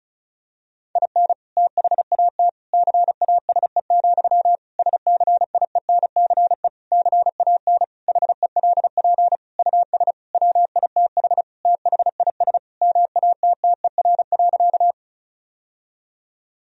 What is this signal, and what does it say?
Morse code, heard as a keyed tone, about 36 wpm, 700 Hz. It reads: IN THAT CASE, SCIENCE CAN HELP US WITH THIS MATTER.
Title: Sentences from Top 1000 Words - Review Part 2 - 35wpm